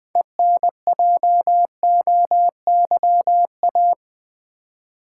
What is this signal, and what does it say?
Morse code, heard as a keyed tone, about 20 wpm, 700 Hz. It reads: ENJOYA
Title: Build A Word - 3 Letters At A Time - 9-Letter Words - 20wpm